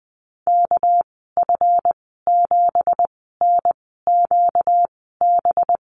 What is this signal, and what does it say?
Morse code, heard as a keyed tone, about 20 wpm, spaced as for 14 wpm, 700 Hz. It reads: KF7NQB
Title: 245 Lesson 39g - Letter z - Call Signs 20wpm